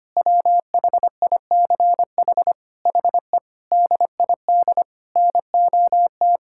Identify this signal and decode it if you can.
Morse code, heard as a keyed tone, about 25 wpm, 700 Hz. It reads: WHICH HE DID NOT
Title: Sentences from Top 1000 Words - Review Part 5 - 25wpm